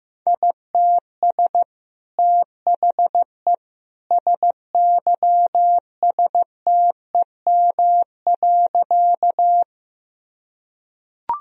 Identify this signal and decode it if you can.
Morse code, heard as a keyed tone, about 16 wpm, 700 Hz. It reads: ITS THE SYSTEM.
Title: Sentences from Top 1000 Words - Review Part 3 - 15wpm